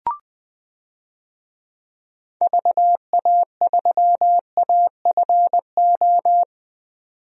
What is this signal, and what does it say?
Morse code, heard as a keyed tone, about 20 wpm, 700 Hz. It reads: VA3AFO
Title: DXCC-100 Call Signs - 6 Characters - 20wpm